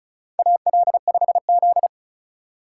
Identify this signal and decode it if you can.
Morse code, heard as a keyed tone, about 36 wpm, 700 Hz. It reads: AL5Z
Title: USA Call Signs - 4x Character Spacing - 35wpm